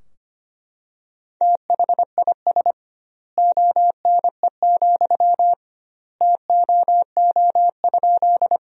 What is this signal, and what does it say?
Morse code, heard as a keyed tone, about 25 wpm, 700 Hz. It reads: THIS ONE, TOO?
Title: 2x Word-Spacing - Sentences from Top 200 Words - 25wpm